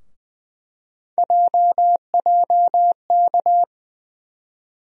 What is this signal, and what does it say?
Morse code, heard as a keyed tone, about 20 wpm, 700 Hz. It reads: JJK
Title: USA Call Sign Postfixes - 3 Letters 20wpm